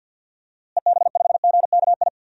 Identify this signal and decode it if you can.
Morse code, heard as a keyed tone, about 50 wpm, 700 Hz. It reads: EB5GKI
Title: DXCC-100 Call Signs - 6 Characters - Repeated 2x - 50wpm